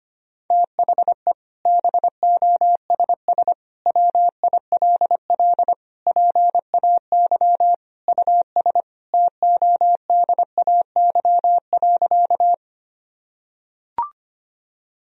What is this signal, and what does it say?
Morse code, heard as a keyed tone, about 25 wpm, 700 Hz. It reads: THE BOSS WILL PAY US TODAY.
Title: Sentences from Top 1000 Words - Review Part 4 - 25wpm